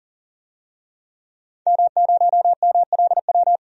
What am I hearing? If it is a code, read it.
Morse code, heard as a keyed tone, about 40 wpm, 700 Hz. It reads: M0MLW
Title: DXCC-100 Call Signs - 5 Characters - 40wpm